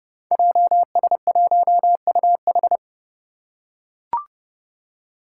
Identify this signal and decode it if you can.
Morse code, heard as a keyed tone, about 30 wpm, 700 Hz. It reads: JS1UH
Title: Five Challenging Letter-Number Combinations 30wpm